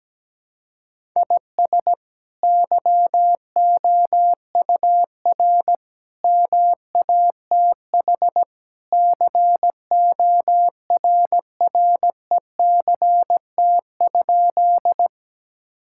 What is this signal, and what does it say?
Morse code, heard as a keyed tone, about 17 wpm, 700 Hz. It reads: IS YOUR MATH CORRECT?
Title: Sentences from Top 1000 Words - Review Part 6 - 17wpm